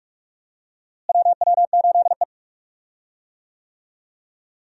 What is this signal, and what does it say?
Morse code, heard as a keyed tone, about 46 wpm, 700 Hz. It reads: WW8E
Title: USA Call Signs - 4 Characters - Repeated 3x - 45wpm